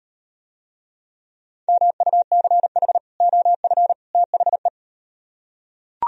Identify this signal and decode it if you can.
Morse code, heard as a keyed tone, about 38 wpm, 700 Hz. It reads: MUCH OF THE